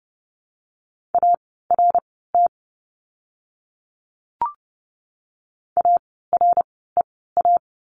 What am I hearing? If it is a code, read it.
Morse code, heard as a keyed tone, about 30 wpm, spaced as for 17 wpm, 700 Hz. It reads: ART AREA
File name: Lesson 17f - Letter u - Words 30wpm